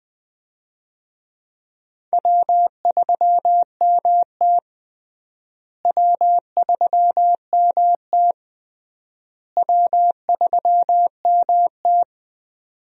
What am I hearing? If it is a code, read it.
Morse code, heard as a keyed tone, about 21 wpm, 700 Hz. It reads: W3MT W3MT W3MT
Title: USA Call Signs - 4 Characters - Repeated 3x - 20wpm